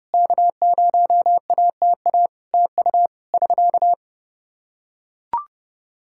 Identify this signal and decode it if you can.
Morse code, heard as a keyed tone, about 30 wpm, 700 Hz. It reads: K0ATA TU <SK>
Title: Practice QSO - Exchange 4 - Ending - 30wpm